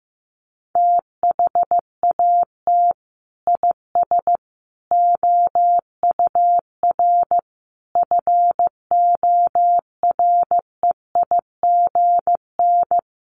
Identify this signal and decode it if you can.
Morse code, heard as a keyed tone, about 15 wpm, 700 Hz. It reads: THAT IS OUR FOREIGN
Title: Sentences from Top 1400 Words 15wpm